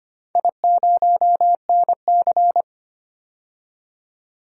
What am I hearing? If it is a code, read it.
Morse code, heard as a keyed tone, about 25 wpm, 700 Hz. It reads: I0NC
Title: DXCC-100 Call Signs - 4 Characters - 25wpm